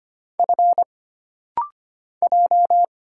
Morse code, heard as a keyed tone, about 25 wpm, 700 Hz. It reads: F J
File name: Single Character (Letter:Number:Punctuation:Prosign) - Rapid-Fire - 25wpm